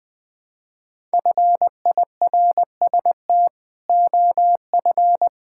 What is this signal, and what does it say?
Morse code, heard as a keyed tone, about 21 wpm, 700 Hz. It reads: FIRST OF
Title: Build A Phrase - 3-Word Phrases - 20wpm